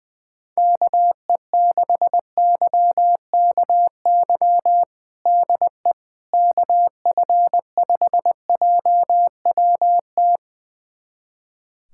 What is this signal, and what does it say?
Morse code, heard as a keyed tone, about 20 wpm, 700 Hz. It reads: KE6YKY DE KF5JWT
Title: USA Callsign DE Callsign - 6 Characters - 20wpm